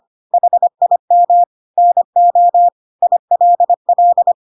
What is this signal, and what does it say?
Morse code, heard as a keyed tone, about 25 wpm, 700 Hz. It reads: HIM NO ILL
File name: Pinocchio / Pinocchio ch2 25 25 ik1hge